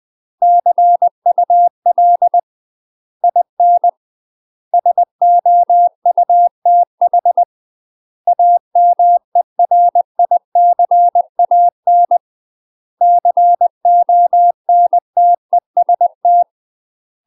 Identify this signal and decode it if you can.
Morse code, heard as a keyed tone, about 20 wpm, 700 Hz. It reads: CUL IN SOUTH AMERICAN CONTEST
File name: UK MASTER QSOs TRAINING / 20WPM / UK MASTER QSOs 0036